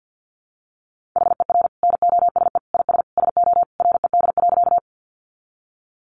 Morse code, heard as a keyed tone, about 50 wpm, 700 Hz. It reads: HER NOSE IS SO RED.
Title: Easy Phrases 50wpm